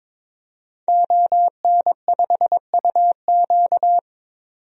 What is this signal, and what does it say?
Morse code, heard as a keyed tone, about 22 wpm, 700 Hz. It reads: ON5UQ
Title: DXCC-100 Call Signs - 5 Characters - 22wpm